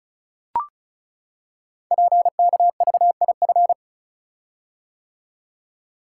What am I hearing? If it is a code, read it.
Morse code, heard as a keyed tone, about 36 wpm, 700 Hz. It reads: PKVIF
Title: Code Groups - 5 Letters - 35wpm